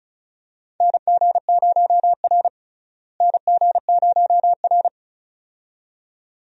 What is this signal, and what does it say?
Morse code, heard as a keyed tone, about 36 wpm, 700 Hz. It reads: NG0R NG0R
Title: USA Call Signs - 4 Characters - Repeated 2x - 35wpm